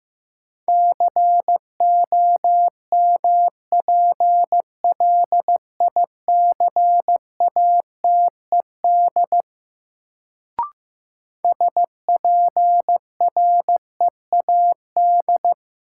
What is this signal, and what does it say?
Morse code, heard as a keyed tone, about 16 wpm, 700 Hz. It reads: COMPLICATED SPREAD
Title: Sentences from Top 1200 Words - New Word Drill - 15wpm